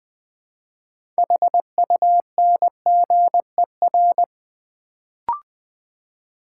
Morse code, heard as a keyed tone, about 20 wpm, 700 Hz. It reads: HUNGER
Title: Build A Word - 3 Letters At A Time - 6-Letter Words - 20wpm